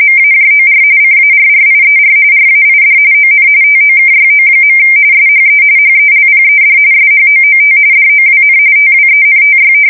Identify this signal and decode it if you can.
RTTY, about 45.45 baud. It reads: MAJOR CARRIER RAID ON TOKYO. SHE HELPED SHOOT DOWN TWENTY
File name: fsk170 / battleshipnj